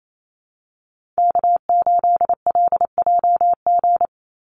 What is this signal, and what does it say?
Morse code, heard as a keyed tone, about 28 wpm, 700 Hz. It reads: K8LJG
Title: NA SOTA Chaser Callsigns 28wpm